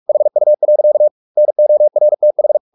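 Morse code, heard as a keyed tone, about 44 wpm, 600 Hz. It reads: HU. NORTH
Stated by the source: Morse practice at 45 wpm, the novel Never by Ken Follett in English, from CW med Gnister.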